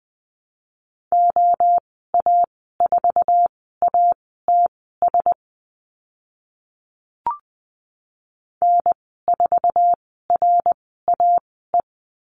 Morse code, heard as a keyed tone, about 20 wpm, spaced as for 15 wpm, 700 Hz. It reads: OA4ATS N4RAE
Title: 076 Lesson 13g - Letter L - Call Signs 20wpm